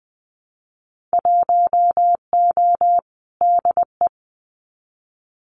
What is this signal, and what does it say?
Morse code, heard as a keyed tone, about 20 wpm, 700 Hz. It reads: 1O DE
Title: Winter Field Day Exchanges 20wpm